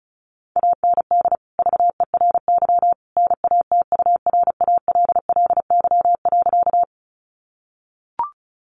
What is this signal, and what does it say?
Morse code, heard as a keyed tone, about 34 wpm, 700 Hz. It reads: AND VERY NATURALLY.
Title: Sentences from Top 1300 Words 35wpm